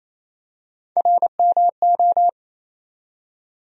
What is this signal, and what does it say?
Morse code, heard as a keyed tone, about 28 wpm, 700 Hz. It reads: RMO
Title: USA Call Sign Postfixes - 3 Letters 28wpm